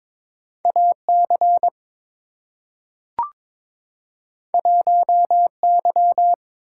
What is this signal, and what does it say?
Morse code, heard as a keyed tone, about 22 wpm, 700 Hz. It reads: AC 1Y
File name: practice / 22wpm / Two Letter-Number Combinations - 22wpm